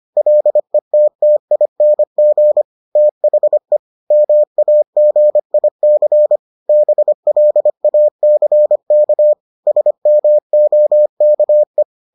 Morse code, heard as a keyed tone, about 25 wpm, 600 Hz. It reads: LETTING THE MAGIC BLACK SMOKE